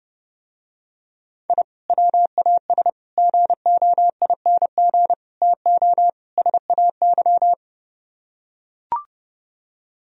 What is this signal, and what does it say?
Morse code, heard as a keyed tone, about 30 wpm, 700 Hz. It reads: I WAS GOING TO SAY